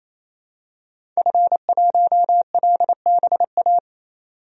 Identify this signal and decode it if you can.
Morse code, heard as a keyed tone, about 28 wpm, 700 Hz. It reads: F1LBA
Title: DXCC-100 Call Signs - 5 Characters - 28wpm